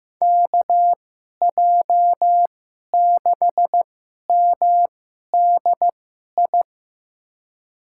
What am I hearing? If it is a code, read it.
Morse code, heard as a keyed tone, about 15 wpm, spaced as for 11 wpm, 700 Hz. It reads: KJ6MDI
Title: USA Call Signs - 6 Characters - 2x Character Spacing - 15wpm